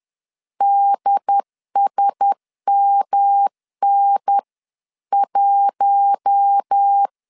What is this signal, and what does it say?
Morse code, heard as a keyed tone, about 10 wpm, 800 Hz. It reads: DSMN1
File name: LEZIONI / 6 PKX0 / 6 P 10wpm 2432